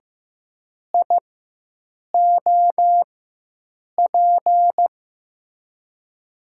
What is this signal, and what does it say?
Morse code, heard as a keyed tone, about 15 wpm, spaced as for 7 wpm, 700 Hz. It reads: IOP
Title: USA Call Signs - 6 Characters - 4x Character Spacing - 15wpm